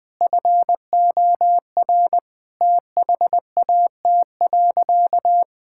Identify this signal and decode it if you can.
Morse code, heard as a keyed tone, about 20 wpm, 700 Hz. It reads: FOR THAT.
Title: Sentences from Top 1000 Words - Review Part 5 - 20wpm